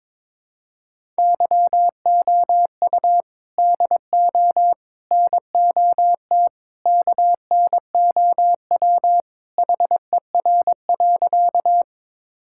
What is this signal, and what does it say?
Morse code, heard as a keyed tone, about 22 wpm, 700 Hz. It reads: YOU DO NOT KNOW HER.
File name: practice / 22wpm / Sentences with 4-Word Phrases - 22wpm